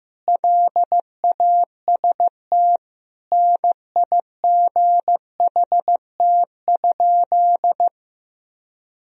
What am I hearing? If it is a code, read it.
Morse code, heard as a keyed tone, about 15 wpm, 700 Hz. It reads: LAST NIGHT?
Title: Sentences from Top 1000 Words - Review Part 1 - 15wpm